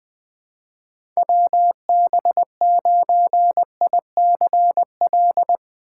Morse code, heard as a keyed tone, about 20 wpm, 700 Hz. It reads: WB9ICL
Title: USA Call Signs - 6 Characters - 2x Character Spacing - 20wpm